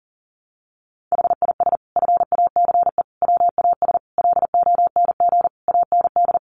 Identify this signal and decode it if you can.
Morse code, heard as a keyed tone, about 40 wpm, 700 Hz. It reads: HIS FACE WAS LONG AND
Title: Sentences from Top 1200 Words 40wpm